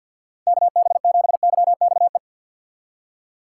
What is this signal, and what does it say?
Morse code, heard as a keyed tone, about 46 wpm, 700 Hz. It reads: KB7YXE
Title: USA Call Signs - 6x Character Spacing - 50wpm